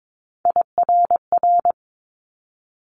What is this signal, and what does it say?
Morse code, heard as a keyed tone, about 22 wpm, 700 Hz. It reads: IRR
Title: Multi-letter Phonemes Quick Fire 22wpm